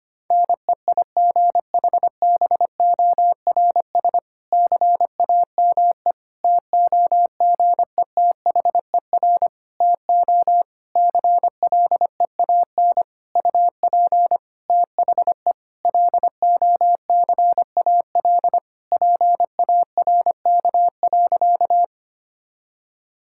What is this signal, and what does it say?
Morse code, heard as a keyed tone, about 25 wpm, 700 Hz. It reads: NEIGHBORS CAME TOGETHER TO CLEAN UP THE LOCAL PARK.
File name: practice / 25wpm / Difficult Sentences with Hints - 25wpm